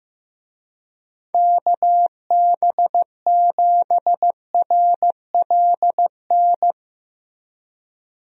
Morse code, heard as a keyed tone, about 16 wpm, 700 Hz. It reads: KB7RLN
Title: USA Call Signs - 6 Characters - Repeated 2x - 15wpm